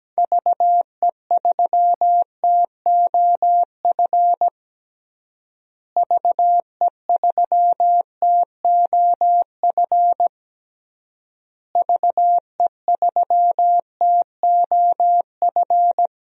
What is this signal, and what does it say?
Morse code, heard as a keyed tone, about 17 wpm, 700 Hz. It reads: VE3TOF VE3TOF VE3TOF
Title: DXCC-100 Call Signs - 6 Characters - Repeated 3x - 17wpm